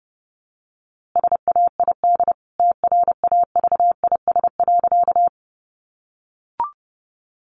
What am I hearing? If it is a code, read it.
Morse code, heard as a keyed tone, about 30 wpm, 700 Hz. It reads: SAID TRAVIS.
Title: A Sound Of Thunder 30wpm